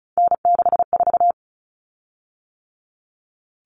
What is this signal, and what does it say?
Morse code, heard as a keyed tone, about 34 wpm, 700 Hz. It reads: N64
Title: Common Cut Numbers - 3-Digit 35wpm